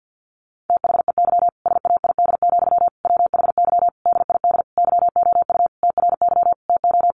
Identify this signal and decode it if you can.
Morse code, heard as a keyed tone, about 49 wpm, 700 Hz. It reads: THEY SAID, WHY DID YOU TRY TO
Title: Sentences from Top 1200 Words 50wpm